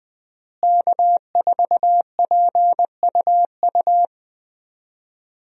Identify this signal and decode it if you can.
Morse code, heard as a keyed tone, about 20 wpm, 700 Hz. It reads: K4PUU
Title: USA Call Signs - 5 Characters - Repeated 3x - 20wpm